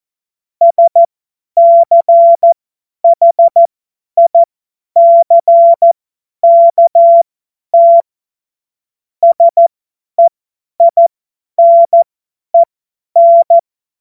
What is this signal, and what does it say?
Morse code, heard as a keyed tone, about 14 wpm, spaced as for 10 wpm, 680 Hz. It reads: SCHICKT SEINEN
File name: Morsekurs / Witze / Witz 02. Dezember 2025 50 F70